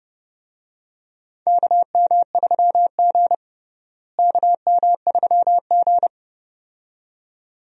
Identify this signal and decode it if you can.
Morse code, heard as a keyed tone, about 30 wpm, 700 Hz. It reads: KM3G KM3G
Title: USA Call Signs - 4 Characters - Repeated 2x - 30wpm